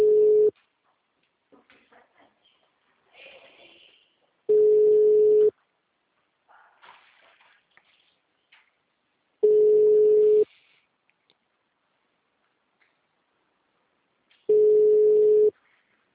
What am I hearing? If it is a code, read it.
Ringback tone (Europe, 425 Hz).